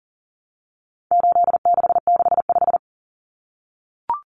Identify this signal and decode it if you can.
Morse code, heard as a keyed tone, about 40 wpm, 700 Hz. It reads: 8665
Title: Numbers - 4-Digit 40wpm